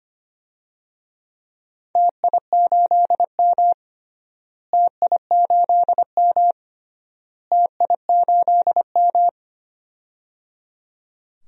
Morse code, heard as a keyed tone, about 25 wpm, 700 Hz. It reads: TI8M TI8M TI8M
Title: DXCC-100 Call Signs - 4 Characters - Repeated 3x - 25wpm